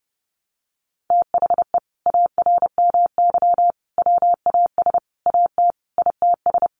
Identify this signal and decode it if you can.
Morse code, heard as a keyed tone, about 30 wpm, 700 Hz. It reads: THE ARMY WAS AT ITS
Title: Sentences from Top 1300 Words 30wpm